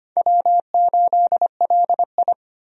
Morse code, heard as a keyed tone, about 25 wpm, 700 Hz. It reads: W8LI
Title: USA Call Signs - 4 Characters - 25wpm